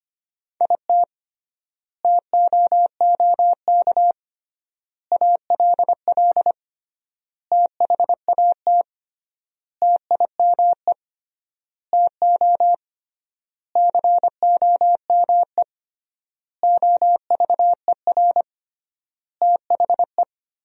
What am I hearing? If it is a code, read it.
Morse code, heard as a keyed tone, about 25 wpm, 700 Hz. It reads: IT TOOK ALL THAT TIME TO COME OVER THE
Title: 3x Word-Spacing - Sentences from Top 300 Words - 25wpm